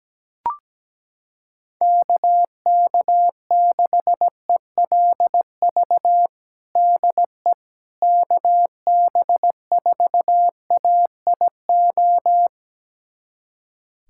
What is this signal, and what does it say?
Morse code, heard as a keyed tone, about 17 wpm, 700 Hz. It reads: KK6ELV DE KB4AIO
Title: USA Callsign DE Callsign - 6 Characters - 17wpm